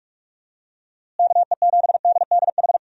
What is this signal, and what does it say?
Morse code, heard as a keyed tone, about 47 wpm, 700 Hz. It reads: KE7DDH
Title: USA Call Signs - 6 Characters - Repeated 3x - 45wpm